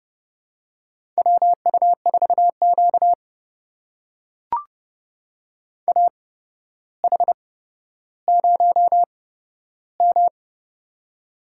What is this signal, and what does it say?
Morse code, heard as a keyed tone, about 30 wpm, spaced as for 13 wpm, 700 Hz. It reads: WU4Q AH0M
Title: USA Call Signs - 4 Characters - 8x Character Spacing - 30wpm